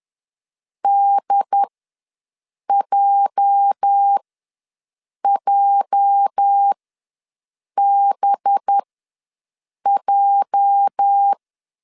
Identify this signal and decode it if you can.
Morse code, heard as a keyed tone, about 10 wpm, spaced as for 6 wpm, 800 Hz. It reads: DJJBJ
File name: LEZIONI / 8 YJ45 / 8 YJ ws 10wpm 4043